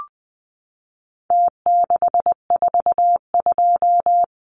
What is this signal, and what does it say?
Morse code, heard as a keyed tone, about 20 wpm, 700 Hz. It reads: T642
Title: Numbers - Contest Style - 4-Digit 20wpm